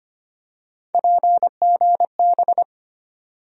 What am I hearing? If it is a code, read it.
Morse code, heard as a keyed tone, about 25 wpm, 700 Hz. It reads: PGB
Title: USA Call Sign Postfixes - 3 Letters 25wpm